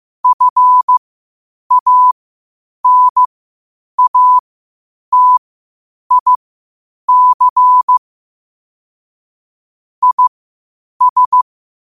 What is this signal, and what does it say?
Morse code, heard as a keyed tone, about 15 wpm, spaced as for 8 wpm, 1000 Hz. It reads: FANATIC IS
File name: qotdfiles / 8wpm